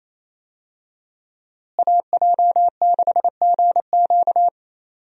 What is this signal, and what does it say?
Morse code, heard as a keyed tone, about 28 wpm, 700 Hz. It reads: AJ6GQ
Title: USA Call Signs - 5 Characters - Repeated 3x - 28wpm